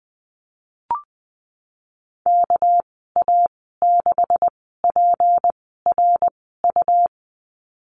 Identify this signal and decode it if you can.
Morse code, heard as a keyed tone, about 20 wpm, spaced as for 15 wpm, 700 Hz. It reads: KA6PRU
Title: 213 Lesson 34g - Number 8 - Call Signs 20wpm